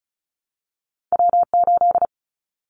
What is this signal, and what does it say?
Morse code, heard as a keyed tone, about 34 wpm, 700 Hz. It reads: W8
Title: Callsign Prefixes for Top 25 DXCC - 35wpm